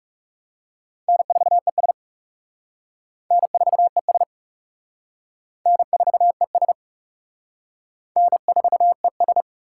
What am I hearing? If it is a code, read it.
Morse code, heard as a keyed tone, about 35 wpm, 700 Hz. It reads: N4ES N4ES N4ES N4ES
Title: Speed Racing - USA Call Signs 30wpm